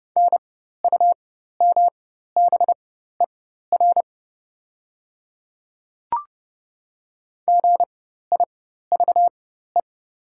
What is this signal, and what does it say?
Morse code, heard as a keyed tone, about 31 wpm, spaced as for 14 wpm, 700 Hz. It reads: NUMBER GIVE
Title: 4x Character Spacing - Top 200 Words - 30wpm